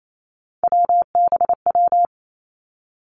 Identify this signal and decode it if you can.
Morse code, heard as a keyed tone, about 28 wpm, 700 Hz. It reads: WBW
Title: Three Challenging Letter-Number Combinations 28wpm